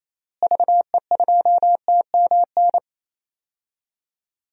Morse code, heard as a keyed tone, about 28 wpm, 700 Hz. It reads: VE2TMN
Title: DXCC-100 Call Signs - 6 Characters - Repeated 2x - 28wpm